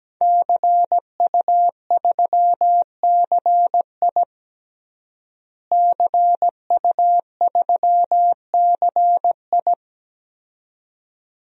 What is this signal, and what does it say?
Morse code, heard as a keyed tone, about 17 wpm, 700 Hz. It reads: CU3CI CU3CI
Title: DXCC-100 Call Signs - 5 Characters - Repeated 2x - 17wpm